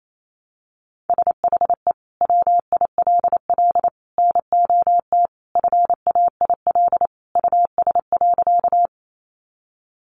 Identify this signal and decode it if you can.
Morse code, heard as a keyed tone, about 28 wpm, 700 Hz. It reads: SHE WILL NOT FAIL US.
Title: Sentences from Top 1100 Words 28wpm